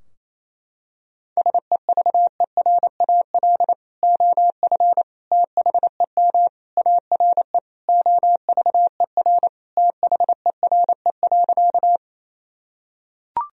Morse code, heard as a keyed tone, about 28 wpm, 700 Hz. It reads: SEVERAL OF THEM ARE OVER THERE.
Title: Sentences from Top 1000 Words - Review Part 2 - 28wpm